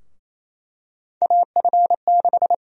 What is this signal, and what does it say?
Morse code, heard as a keyed tone, about 28 wpm, 700 Hz. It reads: AF6
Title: USA Call Sign Prefixes - 28wpm